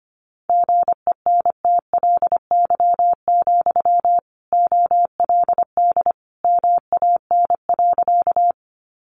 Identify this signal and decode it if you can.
Morse code, heard as a keyed tone, about 25 wpm, 700 Hz. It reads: GENTLY, OLD MAN.
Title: Sentences from Top 1500 Words 25wpm